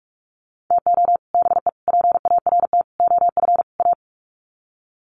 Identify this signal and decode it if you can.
Morse code, heard as a keyed tone, about 44 wpm, 700 Hz. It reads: TO BE PART OF A
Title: Sets of 5 Words Encore2 45wpm